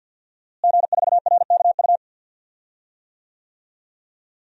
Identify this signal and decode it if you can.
Morse code, heard as a keyed tone, about 46 wpm, 700 Hz. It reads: G4RKU
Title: DXCC-100 Call Signs - 5 Characters - Repeated 2x - 50wpm